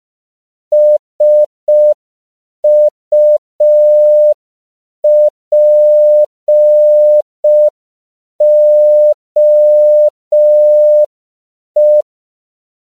Morse code, heard as a keyed tone, about 5 wpm, 600 Hz. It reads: SUPOE
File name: CW / 5PPM